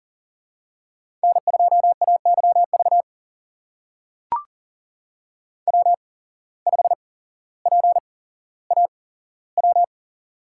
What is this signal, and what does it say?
Morse code, heard as a keyed tone, about 42 wpm, spaced as for 17 wpm, 700 Hz. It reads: N2AYV W5PAW
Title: USA Call Signs - 5 Characters - 8x Character Spacing - 40wpm